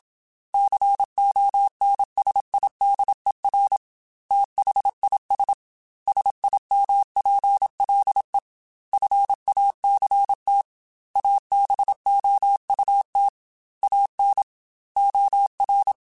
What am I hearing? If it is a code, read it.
Morse code, heard as a keyed tone, about 27 wpm, 800 Hz. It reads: CONSIDER THIS SIMPLE FACT ABOUT AN OR